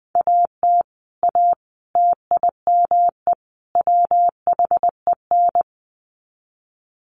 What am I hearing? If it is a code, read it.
Morse code, heard as a keyed tone, about 20 wpm, 700 Hz. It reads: AT A TIME WHEN
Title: Sets of 4 Words 20wpm